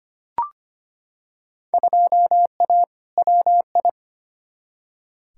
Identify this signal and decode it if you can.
Morse code, heard as a keyed tone, about 25 wpm, 700 Hz. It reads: 2A WI
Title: Field Day Exchanges - 25wpm